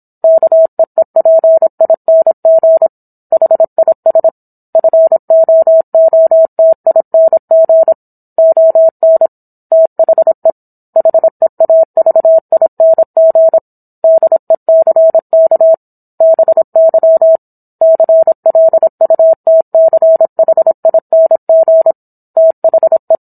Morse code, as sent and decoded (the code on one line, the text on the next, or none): -.- . .--. .. -. --. / .... .. ... / ..-. --- --- - .. -. --. / --- -. / - .... / .... .- ...- .. -. --. / -.. -.-. -.- / -... -.-- / -.-. .-.. ..- - -.-. .... .. -. --. / - .... .
KEEPING HIS FOOTING ON THE HEAVING DECK BY CLUTCHING THE